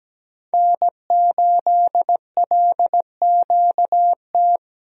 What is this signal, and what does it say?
Morse code, heard as a keyed tone, about 17 wpm, 700 Hz. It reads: N8LQT
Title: USA Call Signs - 5 Characters - 17wpm